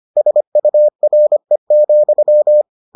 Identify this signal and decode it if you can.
Morse code, heard as a keyed tone, about 25 wpm, 600 Hz. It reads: SURE,